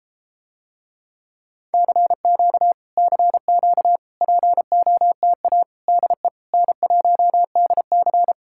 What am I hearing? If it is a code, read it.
Morse code, heard as a keyed tone, about 33 wpm, 700 Hz. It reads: CQ CQ POTA DE N1DC